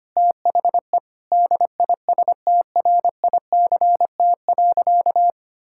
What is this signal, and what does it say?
Morse code, heard as a keyed tone, about 25 wpm, 700 Hz. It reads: THE DISTRICT.
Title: Sentences from Top 1600 Words 25wpm